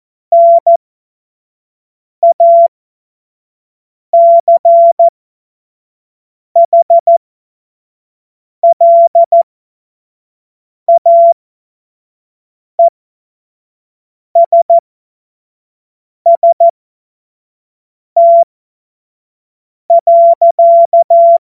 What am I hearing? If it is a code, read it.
Morse code, heard as a keyed tone, about 14 wpm, spaced as for 5 wpm, 680 Hz. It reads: NACHLAESST.
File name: Morsekurs / Witze / Witz 26 März 2024 25 F70